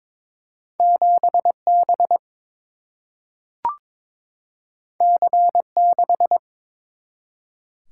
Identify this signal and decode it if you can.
Morse code, heard as a keyed tone, about 22 wpm, 700 Hz. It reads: 7 B C 6
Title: Single-Letter Single-Number 22wpm